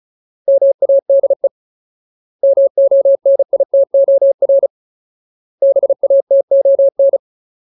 Morse code, heard as a keyed tone, about 34 wpm, 550 Hz. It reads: MADE MONITOR BATON
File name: Ultimate Word List / 35WPM Ultimate